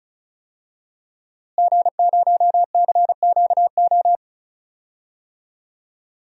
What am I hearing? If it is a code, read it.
Morse code, heard as a keyed tone, about 36 wpm, 700 Hz. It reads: G0CQO
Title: DXCC-100 Call Signs - 5 Characters - Repeated 2x - 35wpm